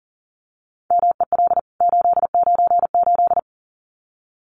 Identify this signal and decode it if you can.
Morse code, heard as a keyed tone, about 40 wpm, 700 Hz. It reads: MEL 898
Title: CWT Exchange 40wpm